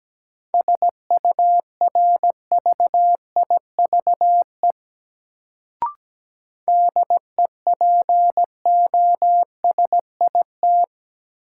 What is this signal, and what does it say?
Morse code, heard as a keyed tone, about 17 wpm, 700 Hz. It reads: SURVIVE DEPOSIT
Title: Sentences from Top 1300 Words - New Word Drill Alt - 17wpm